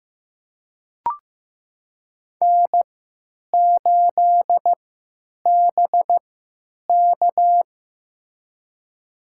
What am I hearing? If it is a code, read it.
Morse code, heard as a keyed tone, about 15 wpm, spaced as for 9 wpm, 700 Hz. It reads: N8BK
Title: USA Call Signs - 4 Characters - 3x Character Spacing - 15wpm